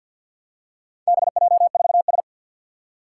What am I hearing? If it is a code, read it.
Morse code, heard as a keyed tone, about 48 wpm, 700 Hz. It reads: BJ4S
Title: Four Challenging Letter-Number Combinations 50wpm